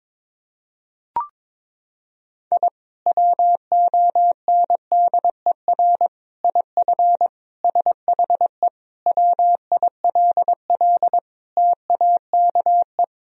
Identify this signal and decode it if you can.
Morse code, heard as a keyed tone, about 22 wpm, 700 Hz. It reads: I WONDER IF SHE WILL TAKE
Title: Sentences from Top 1000 Words - Review Part 3 - 22wpm